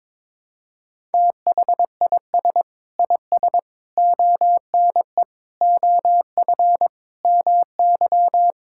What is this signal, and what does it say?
Morse code, heard as a keyed tone, about 22 wpm, 700 Hz. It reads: THIS IS ONE OF MY
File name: practice / 22wpm / Sentences with 4-Word Phrases - 22wpm